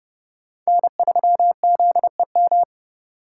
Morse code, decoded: N3ZEM

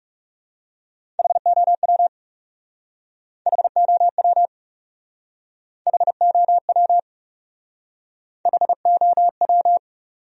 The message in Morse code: HOW HOW HOW HOW

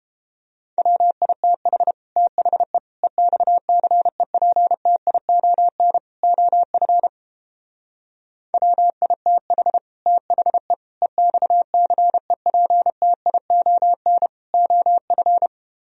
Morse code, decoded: WITH THE EXCEPTION OF WITH THE EXCEPTION OF

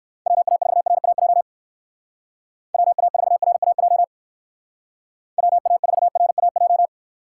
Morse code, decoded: WA4RAJ WA4RAJ WA4RAJ

